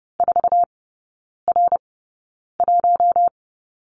Morse code, heard as a keyed tone, about 30 wpm, spaced as for 9 wpm, 700 Hz. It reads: 4R1
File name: course / 30:10wpm / 087 Lesson 15e - Number 5 - Three Characters 30wpm